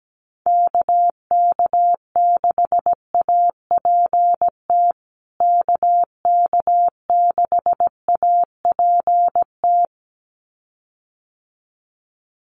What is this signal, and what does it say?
Morse code, heard as a keyed tone, about 17 wpm, 700 Hz. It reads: KK6APT KK6APT
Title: USA Call Signs Repeated 2x 17wpm